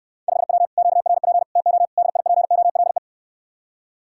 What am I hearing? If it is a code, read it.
Morse code, heard as a keyed tone, about 69 wpm, 700 Hz. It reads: 5W QRP TO DIPOLE